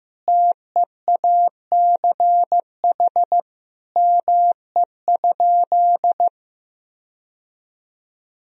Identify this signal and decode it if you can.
Morse code, heard as a keyed tone, about 15 wpm, 700 Hz. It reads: TEACH ME?
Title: Sentences from Top 1000 Words - Review Part 2 - 15wpm